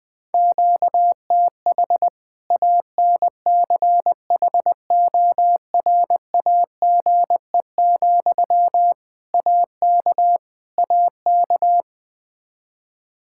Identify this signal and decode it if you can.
Morse code, heard as a keyed tone, about 20 wpm, 700 Hz. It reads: QTH ANCHORAGE, AK AK
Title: K1USN SST QSO Practice 20wpm